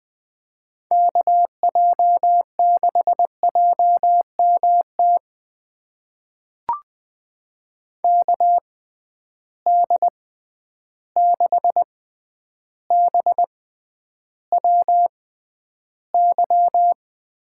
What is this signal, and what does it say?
Morse code, heard as a keyed tone, about 20 wpm, spaced as for 10 wpm, 700 Hz. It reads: KJ6JMT KD6BWY
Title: USA Call Signs - 6 Characters - 6x Character Spacing - 20wpm